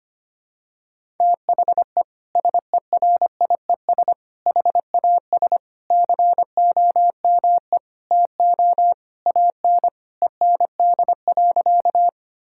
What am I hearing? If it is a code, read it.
Morse code, heard as a keyed tone, about 25 wpm, 700 Hz. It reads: THE SERIES HAS COME TO AN END.
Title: Sentences from Top 1000 Words - Review Part 5 - 25wpm